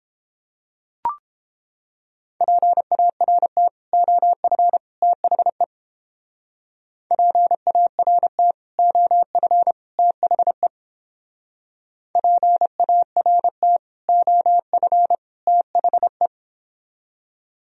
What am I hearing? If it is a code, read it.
Morse code, heard as a keyed tone, about 29 wpm, 700 Hz. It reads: PART OF THE PART OF THE PART OF THE